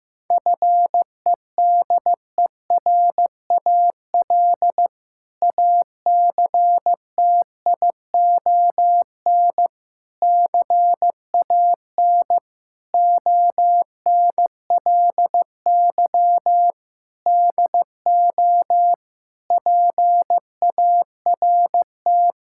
Morse code, heard as a keyed tone, about 15 wpm, 700 Hz. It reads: FEDERAL ACTION CAN ONLY DO PART
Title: Sentences from Top 1000 Words - Review Part 5 - 15wpm